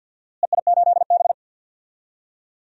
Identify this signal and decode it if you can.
Morse code, heard as a keyed tone, about 48 wpm, 700 Hz. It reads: EI8B
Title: DXCC-100 Call Signs - 4 Characters - Repeated 3x - 50wpm